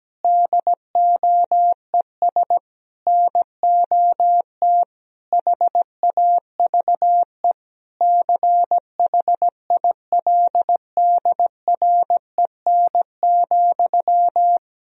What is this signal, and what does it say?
Morse code, heard as a keyed tone, about 17 wpm, 700 Hz. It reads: DOES NOT HAVE CHILDREN,
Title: Sentences from Top 1000 Words - Review Part 3 - 17wpm